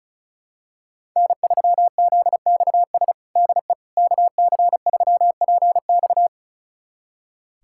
Morse code, heard as a keyed tone, about 36 wpm, 700 Hz. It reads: N3ZXS DE KC3PX
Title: USA Callsign DE Callsign - 5 Characters - 35wpm